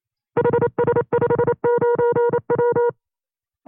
Morse code, heard as a keyed tone, about 28 wpm, 480 Hz. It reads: HS59W
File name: FreeFiles / CW-practice / 25WPM-side2